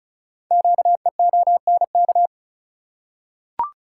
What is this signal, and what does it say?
Morse code, heard as a keyed tone, about 36 wpm, 700 Hz. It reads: QEONK
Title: Code Groups - 5 Letters - 35wpm